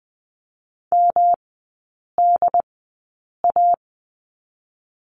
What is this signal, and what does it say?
Morse code, heard as a keyed tone, about 20 wpm, 700 Hz. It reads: M D A
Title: 107 Lesson 18e - Letter m - Three Characters 20wpm